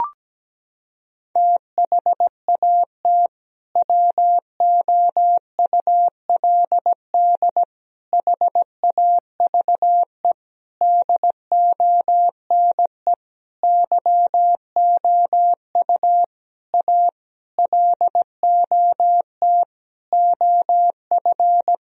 Morse code, heard as a keyed tone, about 17 wpm, 700 Hz. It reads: THAT WOULD HAVE DONE YOU A LOT OF
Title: Sentences from Top 1000 Words - Review Part 3 - 17wpm